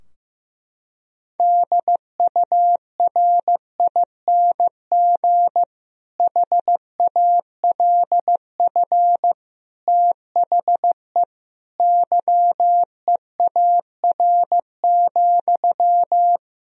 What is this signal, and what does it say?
Morse code, heard as a keyed tone, about 16 wpm, 700 Hz. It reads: DURING HALF THE YEAR,
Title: Sentences from Top 1000 Words - Review Part 2 - 15wpm